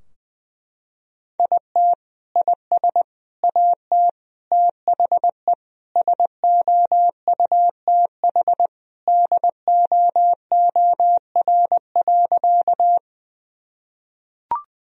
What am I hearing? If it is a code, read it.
Morse code, heard as a keyed tone, about 20 wpm, 700 Hz. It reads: IT IS AT THE SOUTH DOOR.
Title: Sentences from Top 1000 Words - Review Part 2 - 20wpm